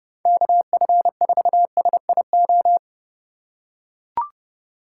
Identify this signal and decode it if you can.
Morse code, heard as a keyed tone, about 31 wpm, 700 Hz. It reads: KF4SIO
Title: USA Call Signs - 3x Character Spacing - 30wpm